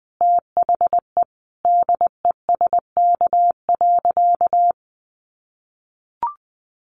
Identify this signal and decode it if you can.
Morse code, heard as a keyed tone, about 20 wpm, 700 Hz. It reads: THE DESK.
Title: A Sound Of Thunder 20wpm